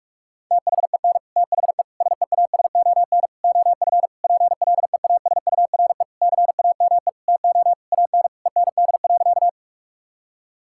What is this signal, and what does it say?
Morse code, heard as a keyed tone, about 46 wpm, 700 Hz. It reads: THEN THE SEASON OF PLEASURE CAME TO AN END.
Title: Sentences from Top 1600 Words 45wpm